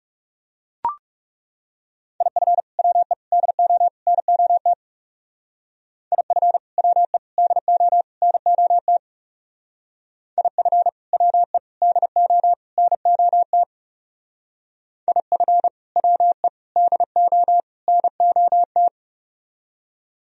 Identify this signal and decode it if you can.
Morse code, heard as a keyed tone, about 38 wpm, 700 Hz. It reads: IF WE DO NOT IF WE DO NOT IF WE DO NOT IF WE DO NOT